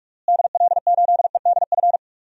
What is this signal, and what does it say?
Morse code, heard as a keyed tone, about 42 wpm, 700 Hz. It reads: DL8EDF
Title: DXCC-100 Call Signs - 6 Characters - Repeated 3x - 45wpm